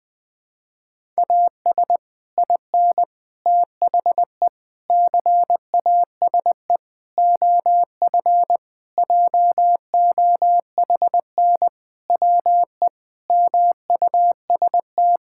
Morse code, heard as a keyed tone, about 20 wpm, 700 Hz. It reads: AS IN THE CASE OF JOHN WE MUST
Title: Sentences with 5-Word Phrases - Spoken Beginning - 20wpm